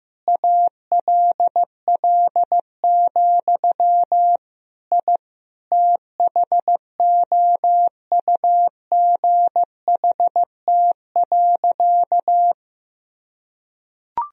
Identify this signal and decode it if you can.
Morse code, heard as a keyed tone, about 15 wpm, 700 Hz. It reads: ALL, I THOUGHT.
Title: Sentences from Top 1000 Words - Review Part 1 - 15wpm